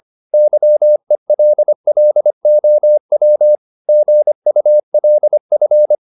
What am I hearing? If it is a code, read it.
Morse code, heard as a keyed tone, about 25 wpm, 600 Hz. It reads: YELLOW GULF